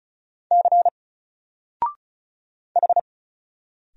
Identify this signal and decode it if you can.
Morse code, heard as a keyed tone, about 37 wpm, 700 Hz. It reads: C H
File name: Single-Letter-Number-35wpm